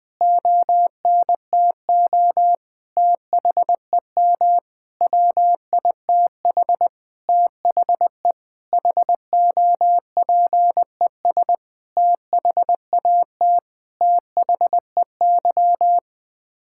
Morse code, heard as a keyed tone, about 20 wpm, 700 Hz. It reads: ONTO THEM WITH THE HOPES THAT THEY
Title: Sets of 9 Words - 20wpm